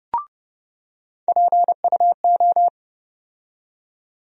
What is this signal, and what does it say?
Morse code, heard as a keyed tone, about 32 wpm, 700 Hz. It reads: PUO
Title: USA Call Sign Postfixes - 3 Letters 30wpm